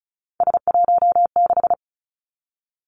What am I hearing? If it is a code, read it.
Morse code, heard as a keyed tone, about 36 wpm, 700 Hz. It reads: S16
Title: Three Challenging Letter-Number Combinations 35wpm